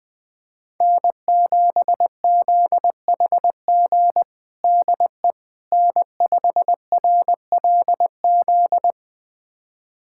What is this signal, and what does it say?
Morse code, heard as a keyed tone, about 20 wpm, 700 Hz. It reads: N7ZHG DE N5RLZ
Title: USA Callsign DE Callsign - 5 Characters - 20wpm